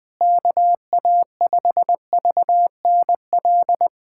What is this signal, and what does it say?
Morse code, heard as a keyed tone, about 20 wpm, 700 Hz. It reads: KA5VNL
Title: USA Callsign DE Callsign - 6 Characters - 20wpm